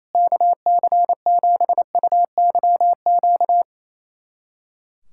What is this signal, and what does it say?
Morse code, heard as a keyed tone, about 27 wpm, 700 Hz. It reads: KC7UYQ
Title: USA Callsign DE Callsign - 6 Characters - 28wpm